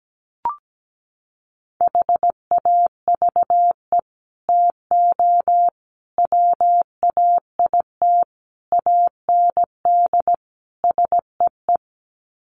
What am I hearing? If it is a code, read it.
Morse code, heard as a keyed tone, about 17 wpm, 700 Hz. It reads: HAVE TO WAIT AND SEE
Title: Sets of 5 Words Encore2 17wpm